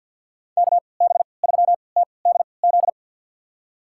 Morse code, heard as a keyed tone, about 45 wpm, spaced as for 27 wpm, 700 Hz. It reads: KB3TDZ
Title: USA Call Signs - 6 Characters - 3x Character Spacing - 50wpm